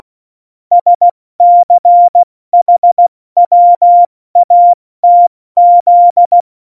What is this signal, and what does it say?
Morse code, heard as a keyed tone, about 16 wpm, spaced as for 14 wpm, 700 Hz. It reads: SCHWATZ
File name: Morsekurs / Witze / Witz 14. August 2018, 70 F80